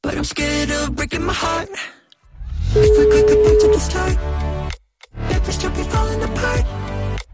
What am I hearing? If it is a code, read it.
Ringback tone (Europe, 425 Hz), with noise.